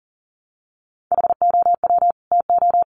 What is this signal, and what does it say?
Morse code, heard as a keyed tone, about 40 wpm, 700 Hz. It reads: HOW TO
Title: Sets of 2 Words 40wpm